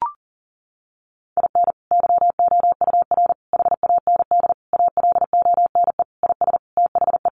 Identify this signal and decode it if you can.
Morse code, heard as a keyed tone, about 37 wpm, 700 Hz. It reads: IN YOUR HAND ALONE IS THE
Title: Sentences from Top 1400 Words 40wpm